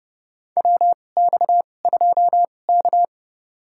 Morse code, heard as a keyed tone, about 30 wpm, spaced as for 22 wpm, 700 Hz. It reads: WX2K
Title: USA Call Signs - 4 Characters - 2x Character Spacing - 30wpm